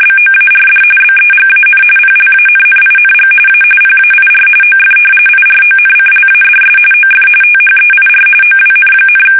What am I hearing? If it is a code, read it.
RTTY, about 45.45 baud. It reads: BOARD BEGAN PLANNING A PAIR OF IRONCLADS OF THEIR OWN,
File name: fsk850 / battleships